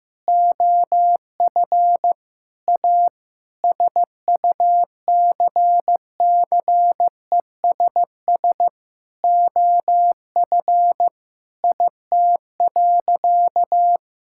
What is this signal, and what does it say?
Morse code, heard as a keyed tone, about 15 wpm, 700 Hz. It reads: OF A SUCCESS OF IT.
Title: Sentences from Top 1000 Words - Review Part 6 - 15wpm